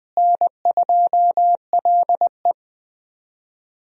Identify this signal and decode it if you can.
Morse code, heard as a keyed tone, about 20 wpm, 700 Hz. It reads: N2LE
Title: USA Call Signs - 4 Characters - Repeated 3x - 20wpm